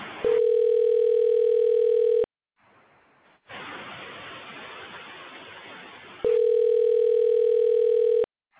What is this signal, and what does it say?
Ringback tone (North America), 440 + 480 Hz.